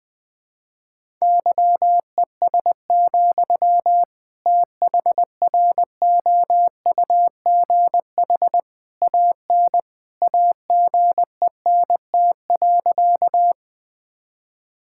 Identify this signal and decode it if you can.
Morse code, heard as a keyed tone, about 21 wpm, 700 Hz. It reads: YES, THROUGH AN AGENT.
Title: Sentences from Top 1600 Words 20wpm